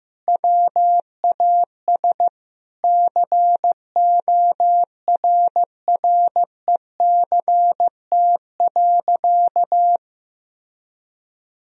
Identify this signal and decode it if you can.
Morse code, heard as a keyed tone, about 15 wpm, 700 Hz. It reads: WAS CORRECT.
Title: Sentences from Top 1000 Words - Review Part 2 - 15wpm